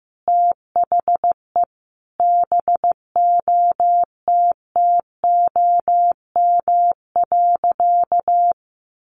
Morse code, heard as a keyed tone, about 15 wpm, 700 Hz. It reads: THE BOTTOM.
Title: Easy Phrases 15wpm